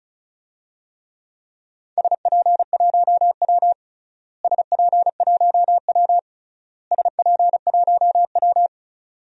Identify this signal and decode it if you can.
Morse code, heard as a keyed tone, about 34 wpm, 700 Hz. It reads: SP1W SP1W SP1W
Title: DXCC-100 Call Signs - 4 Characters - Repeated 3x - 35wpm